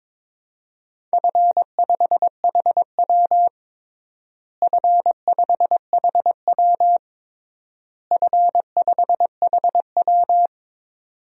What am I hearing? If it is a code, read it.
Morse code, heard as a keyed tone, about 21 wpm, 700 Hz. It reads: F5HW F5HW F5HW
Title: DXCC-100 Call Signs - 4 Characters - Repeated 3x - 22wpm